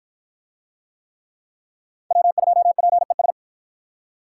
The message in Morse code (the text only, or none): W2PES